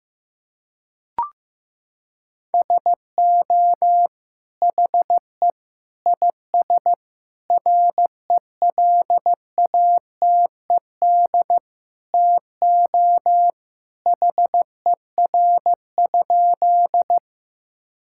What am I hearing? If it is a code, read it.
Morse code, heard as a keyed tone, about 16 wpm, 690 Hz. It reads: SO HE IS RELATED TO HER?
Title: Sentences from Top 1000 Words - Review Part 5 - 15wpm